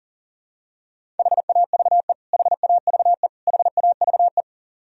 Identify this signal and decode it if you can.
Morse code, heard as a keyed tone, about 40 wpm, 700 Hz. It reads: HAVE HAVE HAVE
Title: Top 100 Words Repeated 40wpm